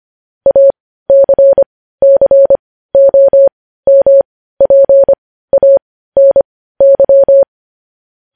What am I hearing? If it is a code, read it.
Morse code, heard as a keyed tone, about 25 wpm, spaced as for 15 wpm, 550 Hz. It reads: ACCOMPANY
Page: news-headlines-in-morse-code-at-15-wpm